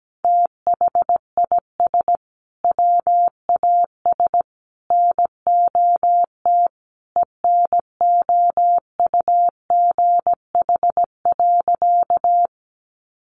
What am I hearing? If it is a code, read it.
Morse code, heard as a keyed tone, about 17 wpm, 700 Hz. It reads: THIS WAS NOT ENOUGH.
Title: Sentences from Top 600 Words 17wpm